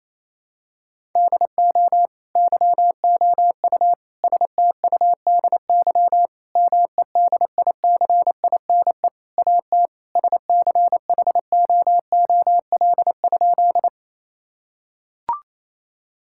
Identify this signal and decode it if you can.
Morse code, heard as a keyed tone, about 28 wpm, 700 Hz. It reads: DO YOU STUDY MEDICINE AT SCHOOL?
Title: Sentences from Top 1000 Words - Review Part 6 - 28wpm